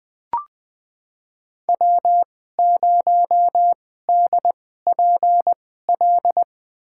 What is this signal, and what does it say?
Morse code, heard as a keyed tone, about 20 wpm, spaced as for 14 wpm, 700 Hz. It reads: W0DPL
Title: USA Call Signs - 5 Characters - 2x Character Spacing - 20wpm